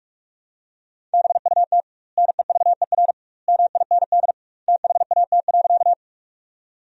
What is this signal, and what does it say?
Morse code, heard as a keyed tone, about 46 wpm, 700 Hz. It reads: BUT NEVER MIND THAT.
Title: 2x Word-Spacing - Sentences from Top 500 Words - 45wpm